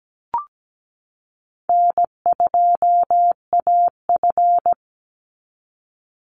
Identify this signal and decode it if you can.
Morse code, heard as a keyed tone, about 17 wpm, 700 Hz. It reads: N2AF
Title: NA SOTA Chaser Callsigns 17wpm